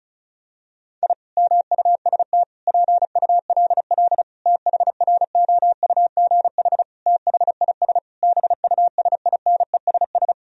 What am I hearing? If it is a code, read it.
Morse code, heard as a keyed tone, about 36 wpm, 700 Hz. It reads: I MUST PULL THROUGH THIS BUSINESS
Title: Sentences from Top 1000 Words - Review Part 2 - 35wpm